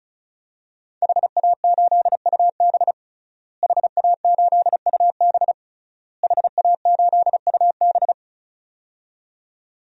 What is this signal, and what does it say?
Morse code, heard as a keyed tone, about 34 wpm, 700 Hz. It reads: HA8UB HA8UB HA8UB
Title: DXCC-100 Call Signs - 5 Characters - Repeated 3x - 35wpm